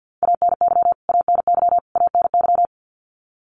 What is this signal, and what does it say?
Morse code, heard as a keyed tone, about 50 wpm, 700 Hz. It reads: ANY ANY ANY
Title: Top 200 Words Repeated 50wpm